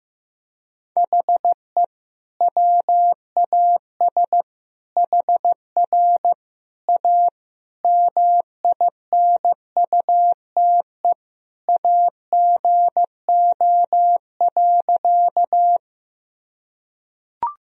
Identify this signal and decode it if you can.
Morse code, heard as a keyed tone, about 14 wpm, 700 Hz. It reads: HE WAS HR A MINUTE AGO.
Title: Sentences from Top 500 Words and Abbreviations 15wpm